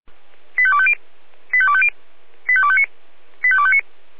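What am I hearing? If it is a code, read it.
Five-tone selective call (ZVEI), heard as quick runs of five tones, tones of about 70 ms. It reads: 85250 85250 85259 85259
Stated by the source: ZVEI five-tone selective call received on an SDR.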